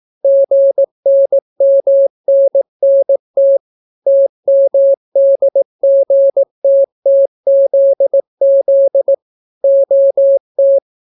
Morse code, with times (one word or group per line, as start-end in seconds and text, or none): 0.2-3.6 GNMNNT
4.1-9.1 TMDGTTZZ
9.6-10.8 OT